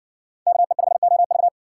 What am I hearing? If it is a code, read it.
Morse code, heard as a keyed tone, about 60 wpm, 700 Hz. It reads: KE5OV